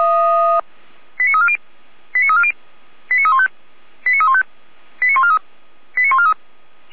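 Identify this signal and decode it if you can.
Five-tone selective call (ZVEI), heard as quick runs of five tones, tones of about 70 ms. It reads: 89340 89340 89315 89315 89143 89143
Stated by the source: ZVEI five-tone selective call received on an SDR.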